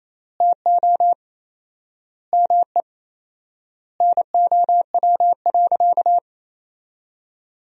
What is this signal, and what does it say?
Morse code, heard as a keyed tone, about 28 wpm, 700 Hz. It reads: TO ME NOW.
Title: 4x Word-Spacing - Sentences from Top 500 Words - 28wpm